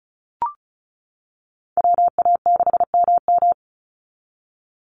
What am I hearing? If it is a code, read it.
Morse code, heard as a keyed tone, about 34 wpm, 700 Hz. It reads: WA6MM
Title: NA SOTA Activator Callsigns 35wpm